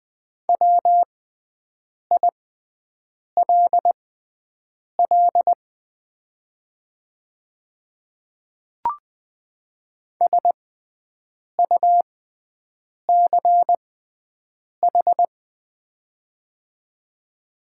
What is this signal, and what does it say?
Morse code, heard as a keyed tone, about 20 wpm, spaced as for 7 wpm, 690 Hz. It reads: WILL SUCH
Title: 6x Character Spacing - Top 300 Words - 20wpm